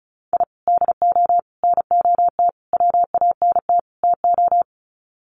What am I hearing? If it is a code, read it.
Morse code, heard as a keyed tone, about 34 wpm, 700 Hz. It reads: I DO NOT WANT TO
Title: Sets of 5 Words 35wpm